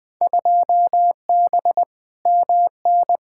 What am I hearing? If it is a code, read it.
Morse code, heard as a keyed tone, about 21 wpm, 700 Hz. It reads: 2B MN
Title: Field Day Exchanges - 20wpm